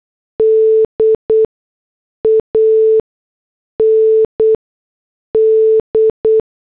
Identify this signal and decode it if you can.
Morse code, heard as a keyed tone, about 8 wpm, spaced as for 6 wpm, 440 Hz. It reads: D AND